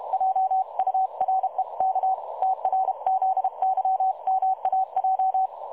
Morse code, heard as a keyed tone, about 44 wpm, spaced as for 35 wpm, 750 Hz. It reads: JUREK TRZYMAJ
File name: dlazaawansowanych / nagraniazpasma / websdr recording 3565.0kHz